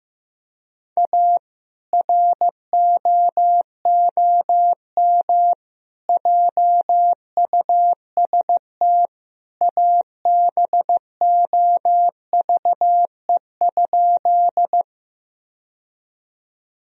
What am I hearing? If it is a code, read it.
Morse code, heard as a keyed tone, about 16 wpm, 700 Hz. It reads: A ROOM JUST ABOVE?
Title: Sentences from Top 1000 Words - Review Part 2 - 15wpm